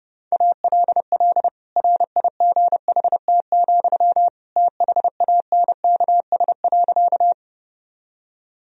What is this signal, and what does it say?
Morse code, heard as a keyed tone, about 30 wpm, 700 Hz. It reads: ALL RIGHT, THANKS.
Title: Sentences from Top 1000 Words - Review Part 3 - 30wpm